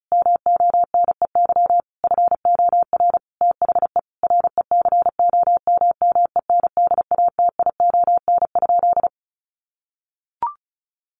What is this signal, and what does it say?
Morse code, heard as a keyed tone, about 34 wpm, 700 Hz. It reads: MONEY FOR THE RECOMMENDATION?
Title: Sentences from Top 1400 Words 35wpm